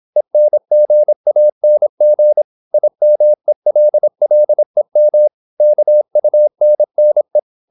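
Morse code, heard as a keyed tone, about 26 wpm, 600 Hz. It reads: ENGANG IMELLEM KUNNE
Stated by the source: High-speed (QRQ) Morse at 26 wpm.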